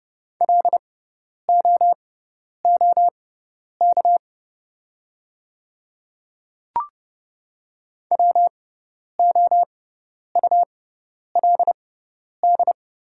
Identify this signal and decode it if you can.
Morse code, heard as a keyed tone, about 30 wpm, spaced as for 10 wpm, 700 Hz. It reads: LOOK WOULD
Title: 6x Character Spacing - Top 300 Words - 30wpm